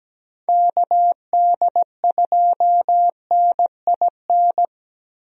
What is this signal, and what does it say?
Morse code, heard as a keyed tone, about 17 wpm, 700 Hz. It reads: KD2NIN
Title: USA Call Signs - 6 Characters - Repeated 3x - 17wpm